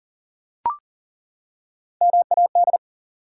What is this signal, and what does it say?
Morse code, heard as a keyed tone, about 40 wpm, 700 Hz. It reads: MAD
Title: Three Letter Words - 40wpm